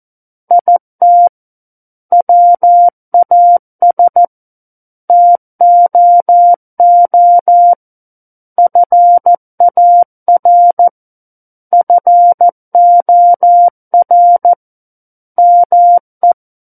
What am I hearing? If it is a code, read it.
Morse code, heard as a keyed tone, about 14 wpm, 700 Hz. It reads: IT WAS TOO FAR FOR ME
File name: Book One / 05-Chapter 5 - 14 WPM